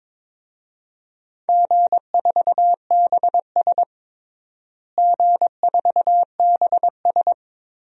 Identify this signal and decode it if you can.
Morse code, heard as a keyed tone, about 22 wpm, 700 Hz. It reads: G4BS G4BS
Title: DXCC-100 Call Signs - 4 Characters - Repeated 3x - 22wpm